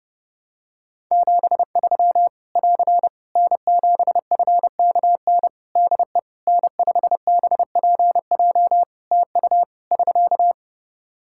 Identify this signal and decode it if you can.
Morse code, heard as a keyed tone, about 28 wpm, 700 Hz. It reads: 73 <AR> N7FKN DE N5BPJ TU <SK>
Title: Practice QSO - Exchange 4 - Ending - 30wpm